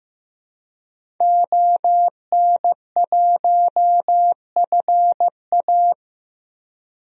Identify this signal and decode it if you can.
Morse code, heard as a keyed tone, about 16 wpm, 700 Hz. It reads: ON1FA
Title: DXCC-100 Call Signs - 5 Characters - Repeated 2x - 15wpm